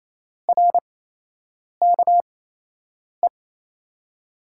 Morse code, heard as a keyed tone, about 27 wpm, spaced as for 8 wpm, 700 Hz. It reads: RKE
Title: USA Call Signs - 6 Characters - 8x Character Spacing - 28wpm